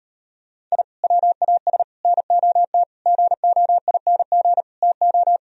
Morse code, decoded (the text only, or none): I WAS NOT GOING TO